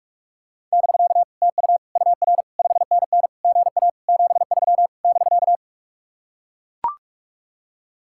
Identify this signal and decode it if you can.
Morse code, heard as a keyed tone, about 44 wpm, 700 Hz. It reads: <BK> TU UR 5NN GA 73 <BK>